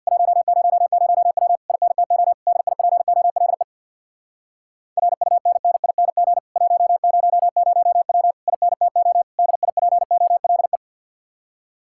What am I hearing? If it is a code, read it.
Morse code, heard as a keyed tone, about 65 wpm, 700 Hz. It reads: 100W INTO DIPOLE RUNNING 100W INTO DIPOLE